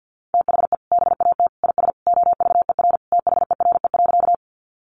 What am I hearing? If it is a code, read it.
Morse code, heard as a keyed tone, about 50 wpm, 700 Hz. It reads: THE BAT IS OVER THERE.
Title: Sentences from Top 1500 Words 50wpm